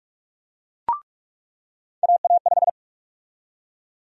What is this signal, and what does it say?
Morse code, heard as a keyed tone, about 41 wpm, 700 Hz. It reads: AA5
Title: USA Call Sign Prefixes - 45wpm